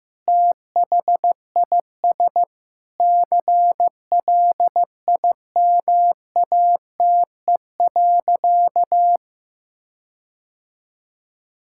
Morse code, heard as a keyed tone, about 15 wpm, 700 Hz. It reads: THIS CLIMATE.
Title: Sentences from Top 1600 Words 15wpm